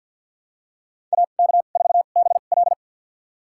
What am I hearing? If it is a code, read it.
Morse code, heard as a keyed tone, about 49 wpm, spaced as for 36 wpm, 700 Hz. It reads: AK4BL
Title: USA Call Signs - 2x Character Spacing - 50wpm